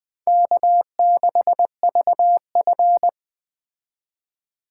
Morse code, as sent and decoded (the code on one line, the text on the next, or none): -.- -.... ...- ..-.
K6VF